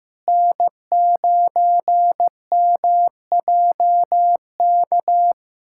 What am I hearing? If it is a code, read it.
Morse code, heard as a keyed tone, about 15 wpm, 700 Hz. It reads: N9MJK
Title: USA Callsign DE Callsign - 5 Characters - 15wpm